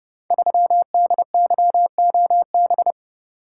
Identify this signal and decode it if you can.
Morse code, heard as a keyed tone, about 30 wpm, 700 Hz. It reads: 3DYOB
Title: Code Groups - 5 Letter:Numbers - 30wpm